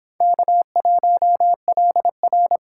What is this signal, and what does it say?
Morse code, heard as a keyed tone, about 23 wpm, 700 Hz. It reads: K1LR